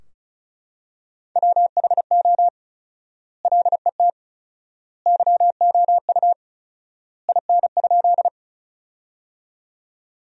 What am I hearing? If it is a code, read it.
Morse code, heard as a keyed tone, about 36 wpm, 700 Hz. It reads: WHO LET YOU IN?
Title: 4x Word-Spacing - Sentences from Top 300 Words - 35wpm